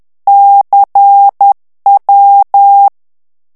Morse code, heard as a keyed tone, about 10 wpm, 800 Hz. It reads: CW
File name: ITALIEN texte clair 2 vitesse 10 mots minute